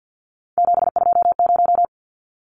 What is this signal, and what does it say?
Morse code, heard as a keyed tone, about 50 wpm, 700 Hz. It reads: 720
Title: Numbers - 3-Digit 50wpm